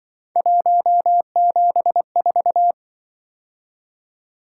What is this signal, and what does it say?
Morse code, heard as a keyed tone, about 24 wpm, 700 Hz. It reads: 174
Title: Speed-Racing - Three Letter-Number Combinations - 20wpm